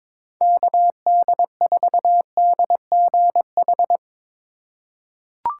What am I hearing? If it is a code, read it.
Morse code, heard as a keyed tone, about 22 wpm, 700 Hz. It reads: KD4DGH
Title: USA Call Signs - 6 Characters - 4x Character Spacing - 22wpm